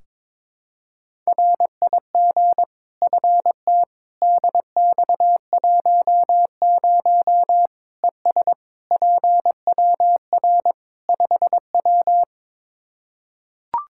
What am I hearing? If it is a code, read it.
Morse code, heard as a keyed tone, about 22 wpm, 700 Hz. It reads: RIG FTDX10 ES PWR 5W
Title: Practice QSO - Exchange 2 - Info - 22wpm